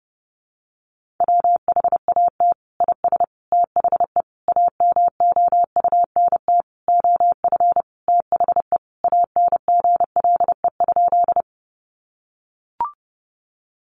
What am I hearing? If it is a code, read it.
Morse code, heard as a keyed tone, about 30 wpm, 700 Hz. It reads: WHAT IS THE AMOUNT OF THE ANGLE?
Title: Sentences from Top 1500 Words 30wpm